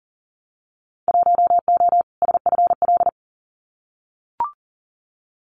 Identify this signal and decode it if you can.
Morse code, heard as a keyed tone, about 40 wpm, 700 Hz. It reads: 1O SFL
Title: Winter Field Day Exchanges Encore 40wpm